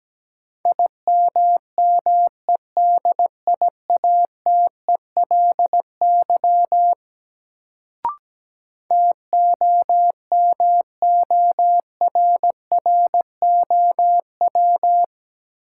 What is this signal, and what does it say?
Morse code, heard as a keyed tone, about 17 wpm, 700 Hz. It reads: IMMEDIATELY TOMORROW
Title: Sentences from Top 1100 Words - New Word Drill Alt - 17wpm